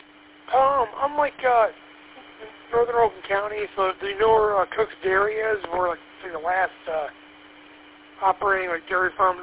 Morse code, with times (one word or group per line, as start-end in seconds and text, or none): none